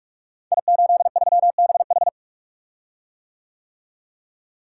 Morse code, heard as a keyed tone, about 46 wpm, 700 Hz. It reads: I83BH
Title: Code Groups - 5 Letter:Numbers - 45wpm